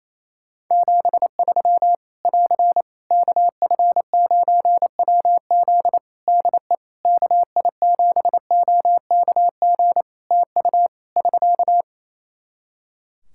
Morse code, heard as a keyed tone, about 28 wpm, 700 Hz. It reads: 73 <AR> KF9WZ DE KI7OKG TU <SK>
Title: K1USN SST QSO Practice 28wpm